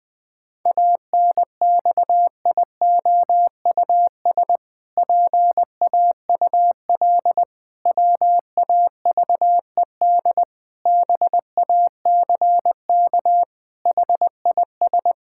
Morse code, heard as a keyed tone, about 21 wpm, 700 Hz. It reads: ANXIOUS PAUL WAVED BACK HIS
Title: Pangram Challenge v2 - 20wpm